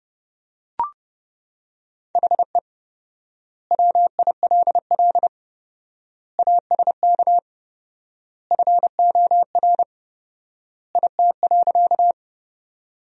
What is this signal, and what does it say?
Morse code, heard as a keyed tone, about 31 wpm, 700 Hz. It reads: HE WILL ASK FOR IT.
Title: 4x Word-Spacing - Sentences from Top 200 Words - 30wpm